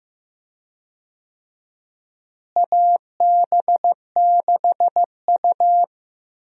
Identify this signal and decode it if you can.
Morse code, heard as a keyed tone, about 16 wpm, 700 Hz. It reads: AB6U
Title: USA Call Signs - 4 Characters - Repeated 3x - 15wpm